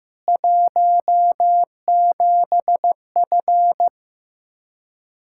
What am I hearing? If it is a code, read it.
Morse code, heard as a keyed tone, about 15 wpm, 700 Hz. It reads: 17F
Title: Practice QSO - Exchange 2 - Info - 15wpm